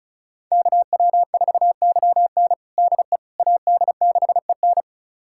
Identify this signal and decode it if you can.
Morse code, heard as a keyed tone, about 36 wpm, 700 Hz. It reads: KW4YN DE AD6EN
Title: USA Callsign DE Callsign - 5 Characters - 35wpm